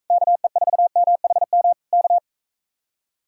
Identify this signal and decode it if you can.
Morse code, heard as a keyed tone, about 45 wpm, 700 Hz. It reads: KE4MHM K